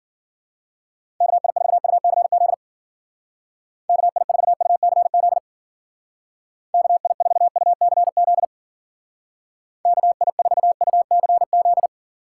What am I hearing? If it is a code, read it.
Morse code, heard as a keyed tone, about 50 wpm, 700 Hz. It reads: KI4UCZ KI4UCZ KI4UCZ KI4UCZ